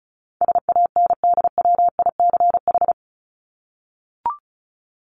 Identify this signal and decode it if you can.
Morse code, heard as a keyed tone, about 34 wpm, 700 Hz. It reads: SANDWICH
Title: Top 1751-2000 Words 35wpm